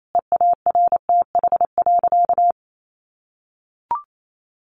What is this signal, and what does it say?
Morse code, heard as a keyed tone, about 28 wpm, 700 Hz. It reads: EARTH.
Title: Sentences from Top 1200 Words 28wpm